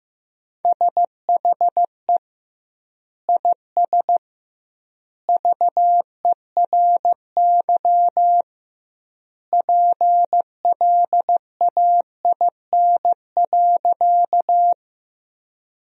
Morse code, heard as a keyed tone, about 15 wpm, 700 Hz. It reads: SHE IS VERY PLAIN.
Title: 2x Word-Spacing - Sentences from Top 400 Words - 15wpm